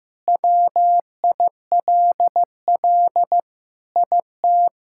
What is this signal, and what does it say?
Morse code, heard as a keyed tone, about 16 wpm, 700 Hz. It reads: WILL IT
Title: Sentences from Top 1000 Words - Review Part 6 - 15wpm